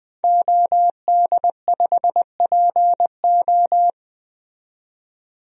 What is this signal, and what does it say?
Morse code, heard as a keyed tone, about 21 wpm, 700 Hz. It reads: OD5PO
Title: DXCC-100 Call Signs - 5 Characters - 20wpm